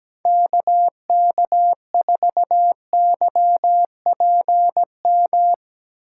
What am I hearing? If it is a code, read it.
Morse code, heard as a keyed tone, about 17 wpm, 700 Hz. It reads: KK4YPM
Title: USA Call Signs - 6 Characters - 8x Character Spacing - 17wpm